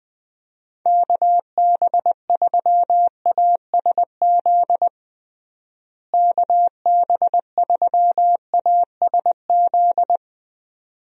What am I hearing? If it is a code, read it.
Morse code, heard as a keyed tone, about 21 wpm, 700 Hz. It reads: KB3ASZ KB3ASZ
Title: USA Call Signs - 6 Characters - Repeated 2x - 20wpm